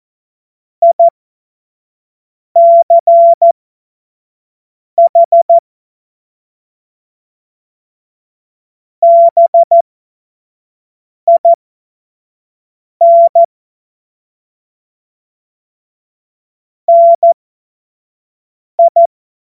Morse code, heard as a keyed tone, about 14 wpm, spaced as for 5 wpm, 680 Hz. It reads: ICH BIN NI
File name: Morsekurs / Witze / Witz 01. März 2023 25 F70